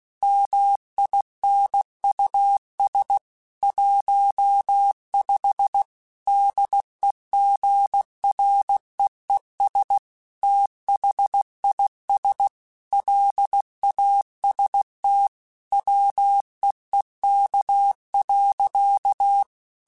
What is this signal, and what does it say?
Morse code, heard as a keyed tone, about 16 wpm, 800 Hz. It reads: MINUS 15 DEGREES THIS LAST WEEK.